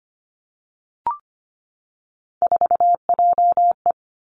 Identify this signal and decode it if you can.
Morse code, heard as a keyed tone, about 25 wpm, 700 Hz. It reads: 4JE
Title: Three Challenging Letter-Number Combinations 25wpm